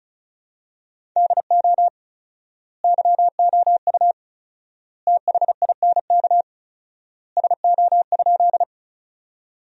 Morse code, heard as a keyed tone, about 34 wpm, 700 Hz. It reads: DO YOU THINK SO?
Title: 4x Word-Spacing - Sentences from Top 200 Words - 35wpm